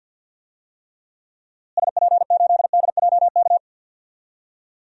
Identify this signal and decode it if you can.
Morse code, heard as a keyed tone, about 52 wpm, 700 Hz. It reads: SP8DJK
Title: DXCC-100 Call Signs - 6 Characters - Repeated 3x - 50wpm